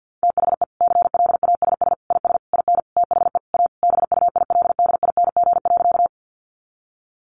Morse code, heard as a keyed tone, about 50 wpm, 700 Hz. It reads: THE CLASS IS IN THE A BUILDING.
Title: Additional Easy Phrases 50wpm